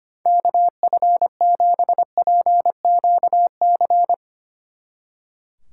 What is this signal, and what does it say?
Morse code, heard as a keyed tone, about 29 wpm, 700 Hz. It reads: KF7PQC
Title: Speed Racing - USA 2x3 Call Signs 25wpm